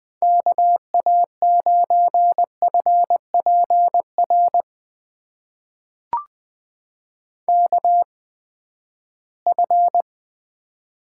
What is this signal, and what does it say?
Morse code, heard as a keyed tone, about 20 wpm, spaced as for 14 wpm, 700 Hz. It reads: KA9FPR KF
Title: USA Call Signs - 6 Characters - 8x Character Spacing - 20wpm